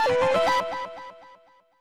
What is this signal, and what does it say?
There is no code.